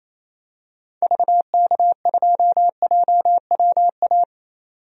Morse code, heard as a keyed tone, about 28 wpm, 700 Hz. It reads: VK2JWA
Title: DXCC-100 Call Signs - 6 Characters - 28wpm